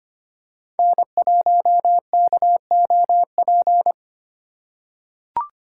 Morse code, heard as a keyed tone, about 25 wpm, 700 Hz. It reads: N1KOP
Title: USA Call Signs - 2x Character Spacing - 25wpm